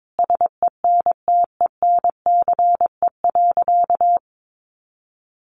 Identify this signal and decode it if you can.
Morse code, heard as a keyed tone, about 22 wpm, 700 Hz. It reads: SENTENCE.
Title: Additional Easy Phrases 22wpm